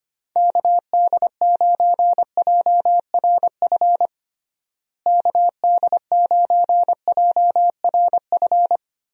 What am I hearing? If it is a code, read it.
Morse code, heard as a keyed tone, about 25 wpm, 700 Hz. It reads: KD9JRF KD9JRF
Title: USA Call Signs - 6 Characters - Repeated 3x - 25wpm